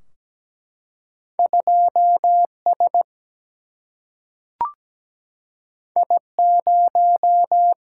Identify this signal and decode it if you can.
Morse code, heard as a keyed tone, about 17 wpm, 700 Hz. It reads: 2 S I 0
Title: Single-Letter Single-Number 17wpm